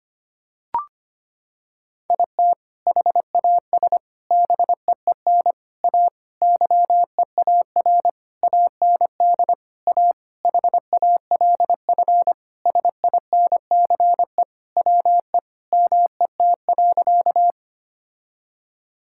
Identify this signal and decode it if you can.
Morse code, heard as a keyed tone, about 25 wpm, 700 Hz. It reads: IT HAS BEEN A YEAR AND A HALF SINCE WE MET.
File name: practice / 25wpm / Sentences with 4-Word Phrases - 25wpm